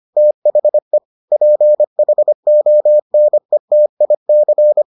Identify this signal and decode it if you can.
Morse code, heard as a keyed tone, about 25 wpm, 600 Hz. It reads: THE PHONETIC